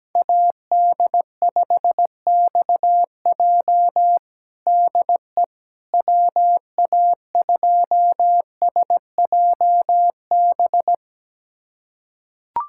Morse code, heard as a keyed tone, about 17 wpm, 700 Hz. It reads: AD5XJ DE WA2SJB
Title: USA Callsign DE Callsign 17wpm